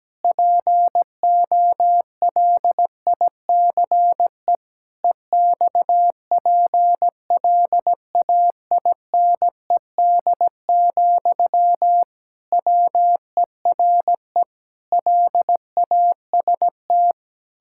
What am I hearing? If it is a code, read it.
Morse code, heard as a keyed tone, about 17 wpm, 700 Hz. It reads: POLICE EXPLAINED, WERE LAST
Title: Pangram Challenge v2 - 17wpm